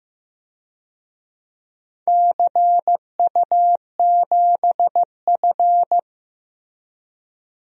Morse code, heard as a keyed tone, about 15 wpm, 700 Hz. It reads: CU7F
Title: DXCC-100 Call Signs - 4 Characters - Repeated 2x - 15wpm